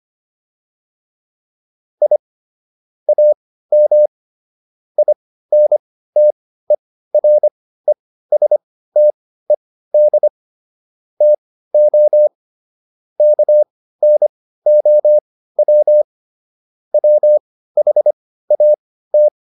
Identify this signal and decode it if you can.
Morse code, heard as a keyed tone, about 25 wpm, spaced as for 15 wpm, 600 Hz. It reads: I AM INTERESTED TO KNOW WHAT
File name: QSO409 15